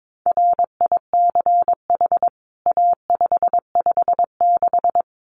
Morse code, heard as a keyed tone, about 22 wpm, 700 Hz. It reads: RICH A556
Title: CWT Exchange 22wpm